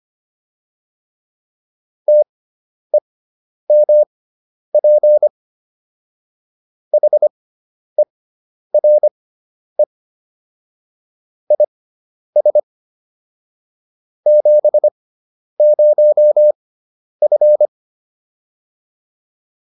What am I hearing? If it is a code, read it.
Morse code, heard as a keyed tone, about 25 wpm, spaced as for 10 wpm, 600 Hz. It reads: TEMP HERE IS 70F